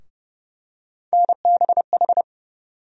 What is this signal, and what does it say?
Morse code, heard as a keyed tone, about 30 wpm, 700 Hz. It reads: NBH
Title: USA Call Sign Postfixes - 3 Letters 30wpm